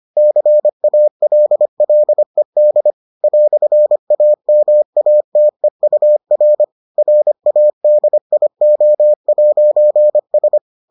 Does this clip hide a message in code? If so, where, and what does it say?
Yes: Morse code from 0.2 to 10.6 s, heard as a keyed tone, about 25 wpm, 600 Hz. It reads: CALLED "AMATEUR RADIO'S